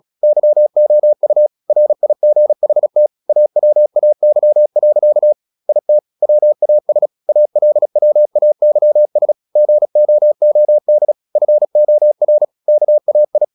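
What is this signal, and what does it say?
Morse code, heard as a keyed tone, about 36 wpm, 600 Hz. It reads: YOU RIGHT AWAY. IT WAS ALWAYS GOOD FOR KAI
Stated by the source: Morse practice at 36 wpm, text of Ken Follett's novel Never, from CW med Gnister.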